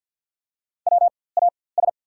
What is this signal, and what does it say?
Morse code, heard as a keyed tone, about 52 wpm, spaced as for 24 wpm, 690 Hz. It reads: WAS
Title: 4x Character Spacing - Top 300 Words - 50wpm